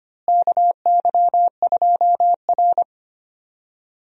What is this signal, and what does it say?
Morse code, heard as a keyed tone, about 25 wpm, 700 Hz. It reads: KY2R
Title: USA Call Signs - 4 Characters - Repeated 3x - 25wpm